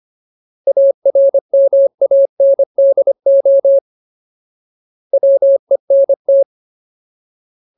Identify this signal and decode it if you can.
Morse code, heard as a keyed tone, about 25 wpm, 550 Hz. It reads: ARMANDO WENT
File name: Ultimate Word List / 25WPM Ultimate